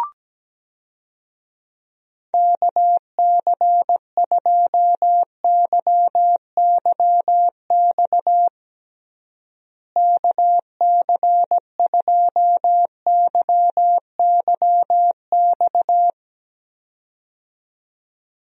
Morse code, heard as a keyed tone, about 17 wpm, 700 Hz. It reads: KC2YYX KC2YYX
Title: USA Call Signs - 6 Characters - Repeated 2x - 17wpm